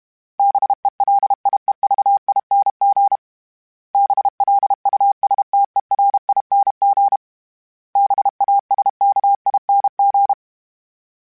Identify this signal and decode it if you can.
Morse code, heard as a keyed tone, about 32 wpm, 800 Hz. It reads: BELIEVING BLUSTERING BASKING
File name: Special INGs-mp3s / Special INGs-mp3s / Sngl Ing @30